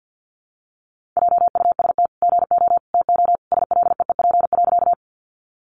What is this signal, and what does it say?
Morse code, heard as a keyed tone, about 50 wpm, 700 Hz. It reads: JUST GO TO SLEEP.
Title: Sentences from Top 500 Words 50wpm